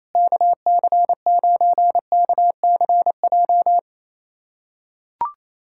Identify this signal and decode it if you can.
Morse code, heard as a keyed tone, about 27 wpm, 700 Hz. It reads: KC9KCJ
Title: USA Callsign DE Callsign 28wpm